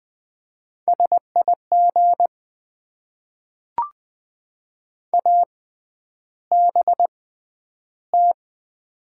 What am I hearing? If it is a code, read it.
Morse code, heard as a keyed tone, about 20 wpm, spaced as for 10 wpm, 700 Hz. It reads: SIG ABT
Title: Top 90 QSO Elements - 6x Character Spacing - 20wpm